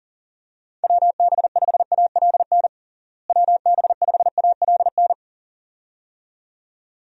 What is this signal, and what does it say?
Morse code, heard as a keyed tone, about 40 wpm, 700 Hz. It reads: WB5ALN WB5ALN
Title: USA Call Signs - 6 Characters - Repeated 2x - 40wpm